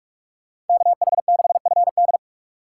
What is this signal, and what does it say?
Morse code, heard as a keyed tone, about 46 wpm, 700 Hz. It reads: KH6FD
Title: USA Call Signs - 4x Character Spacing - 45wpm